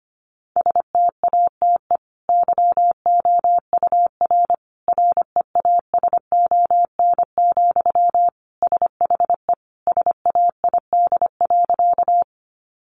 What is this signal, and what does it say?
Morse code, heard as a keyed tone, about 25 wpm, 700 Hz. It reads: STATE YOUR REASON, SHE SAID.
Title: Sentences from Top 700 Words 25wpm